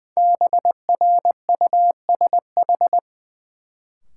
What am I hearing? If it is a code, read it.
Morse code, heard as a keyed tone, about 20 wpm, 700 Hz. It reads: BRUSH
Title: Sentences from Top 1100 Words - New Word Drill Alt - 20wpm